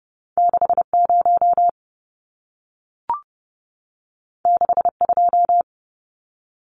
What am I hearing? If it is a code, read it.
Morse code, heard as a keyed tone, about 28 wpm, 700 Hz. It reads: 60 62
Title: Numbers - 2-Digit 30wpm